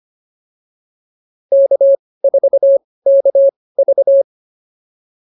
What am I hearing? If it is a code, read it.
Morse code, heard as a keyed tone, about 25 wpm, spaced as for 18 wpm, 550 Hz. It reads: K4KV